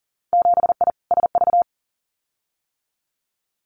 Morse code, heard as a keyed tone, about 37 wpm, 700 Hz. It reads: 7I SV
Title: Winter Field Day Exchanges Encore 40wpm